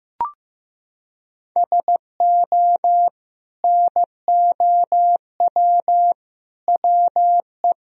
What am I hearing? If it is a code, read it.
Morse code, heard as a keyed tone, about 15 wpm, 700 Hz. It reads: SO NOW WE
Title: Sentences from Top 1000 Words - Review Part 2 - 15wpm